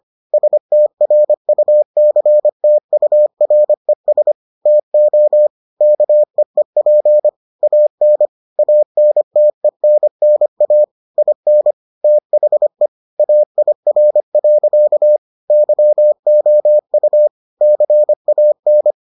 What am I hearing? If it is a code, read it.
Morse code, heard as a keyed tone, about 25 wpm, 600 Hz. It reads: STRUCTURES TO KEEP AN ANTENNA IN THE AIR. YOU CAN